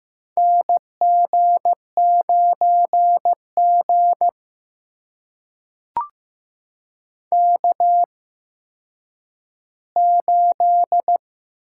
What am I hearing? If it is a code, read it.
Morse code, heard as a keyed tone, about 15 wpm, spaced as for 9 wpm, 700 Hz. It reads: NG9G K8
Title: USA Call Signs - 4 Characters - 8x Character Spacing - 15wpm